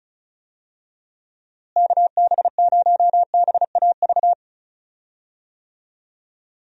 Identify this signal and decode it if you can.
Morse code, heard as a keyed tone, about 36 wpm, 700 Hz. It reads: KB0BAV
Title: USA Call Signs - 6 Characters - Repeated 2x - 35wpm